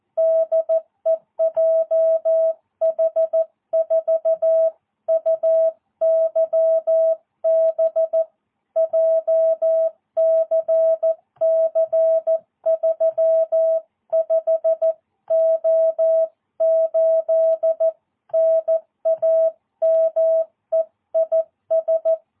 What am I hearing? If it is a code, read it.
Morse code, heard as a keyed tone, about 14 wpm, 650 Hz. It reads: DE JH4UYB JCC 35O8 NAME IS
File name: jh8cbh / 2022cw